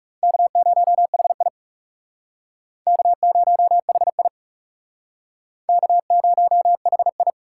Morse code, heard as a keyed tone, about 39 wpm, 700 Hz. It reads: K0HI K0HI K0HI